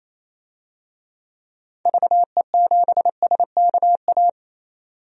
Morse code, heard as a keyed tone, about 28 wpm, 700 Hz. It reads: VE7SKA
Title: DXCC-100 Call Signs - 6 Characters - 28wpm